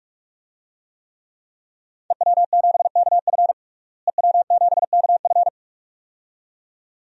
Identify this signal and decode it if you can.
Morse code, heard as a keyed tone, about 41 wpm, 700 Hz. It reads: EW7KF EW7KF
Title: DXCC-100 Call Signs - 5 Characters - Repeated 2x - 45wpm